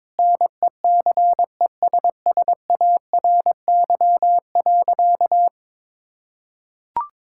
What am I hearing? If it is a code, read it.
Morse code, heard as a keyed tone, about 22 wpm, 700 Hz. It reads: NECESSARY.
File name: Sentences with 9-Word Phrases - 22wpm